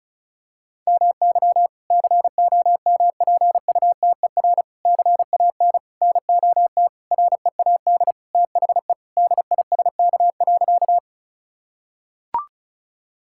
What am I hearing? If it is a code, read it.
Morse code, heard as a keyed tone, about 36 wpm, 700 Hz. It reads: MY COMPUTER CAN NOT READ THE DISK.
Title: Sentences from Top 1000 Words - Review Part 6 - 35wpm